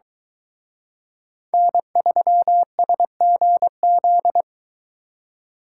Morse code, heard as a keyed tone, about 22 wpm, 700 Hz. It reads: N3SGZ